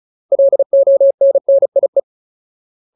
Morse code, heard as a keyed tone, about 34 wpm, 550 Hz. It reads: LONNIE
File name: Ultimate Word List / 35WPM Ultimate